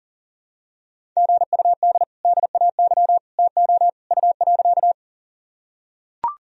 Morse code, heard as a keyed tone, about 40 wpm, 700 Hz. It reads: GUD DAY TO U.
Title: Sentences from Top 200 Words and Abbreviations 40wpm